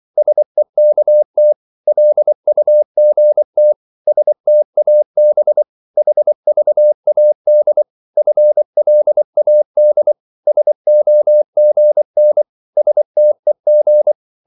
Morse code, heard as a keyed tone, about 24 wpm, 600 Hz. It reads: SEKT LUGT STAB HVAD FLAD SOGN STEG
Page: Korte danske ord 24wpm